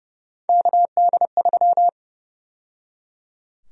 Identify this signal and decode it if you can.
Morse code, heard as a keyed tone, about 30 wpm, 700 Hz. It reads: KD3
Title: USA Call Sign Prefixes - 30wpm